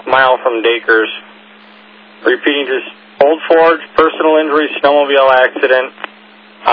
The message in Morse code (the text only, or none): none